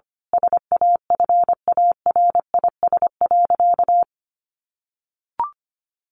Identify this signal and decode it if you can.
Morse code, heard as a keyed tone, about 25 wpm, 700 Hz. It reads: SAFARIS.
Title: A Sound Of Thunder 25wpm